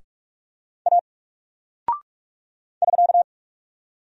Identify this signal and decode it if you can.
Morse code, heard as a keyed tone, about 46 wpm, 700 Hz. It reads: A <SK>
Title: Single Character (Letter:Number:Punctuation:Prosign) Alt 45wpm